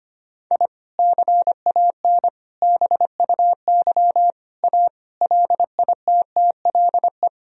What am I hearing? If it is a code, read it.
Morse code, heard as a keyed tone, about 25 wpm, 700 Hz. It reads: I CAN BUY A LITTLE
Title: Sentences from Top 1000 Words - Review Part 1 - 25wpm